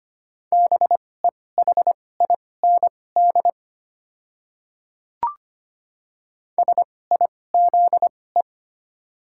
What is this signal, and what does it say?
Morse code, heard as a keyed tone, about 25 wpm, spaced as for 18 wpm, 700 Hz. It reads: BEHIND SIZE
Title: 2x Character Spacing - Top 500 Words - 25wpm